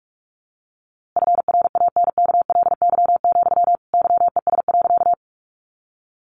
Morse code, heard as a keyed tone, about 45 wpm, 700 Hz. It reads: FRANKLY, YES.
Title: A Sound Of Thunder 45wpm